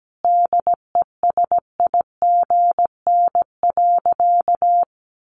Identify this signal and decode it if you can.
Morse code, heard as a keyed tone, about 17 wpm, 700 Hz. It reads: DESIGN.
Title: Sentences from Top 800 Words 17wpm